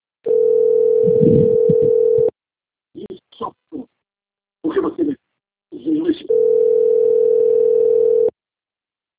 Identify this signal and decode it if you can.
Ringback tone (North America), 440 + 480 Hz, with speech.
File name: rg-1600-unknown- - - .41674